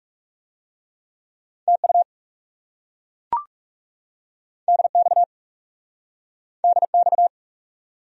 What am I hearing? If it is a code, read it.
Morse code, heard as a keyed tone, about 40 wpm, 700 Hz. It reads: TU DX DX